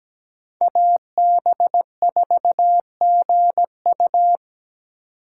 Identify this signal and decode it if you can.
Morse code, heard as a keyed tone, about 17 wpm, 700 Hz. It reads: AB4GU
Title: USA Call Signs - 5 Characters - 2x Character Spacing - 17wpm